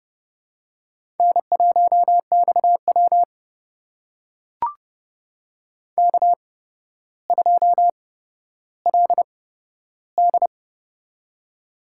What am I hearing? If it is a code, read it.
Morse code, heard as a keyed tone, about 29 wpm, spaced as for 13 wpm, 700 Hz. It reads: N1XW K2LD
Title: USA Call Signs - 4 Characters - 8x Character Spacing - 30wpm